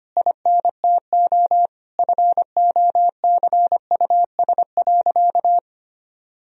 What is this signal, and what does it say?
Morse code, heard as a keyed tone, about 25 wpm, 700 Hz. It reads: INTO FOCUS.
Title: Sentences from Top 1000 Words - Review Part 4 - 25wpm